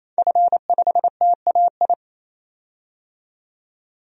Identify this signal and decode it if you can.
Morse code, heard as a keyed tone, about 28 wpm, 700 Hz. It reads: F5TAI
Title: DXCC-100 Call Signs - 5 Characters - 28wpm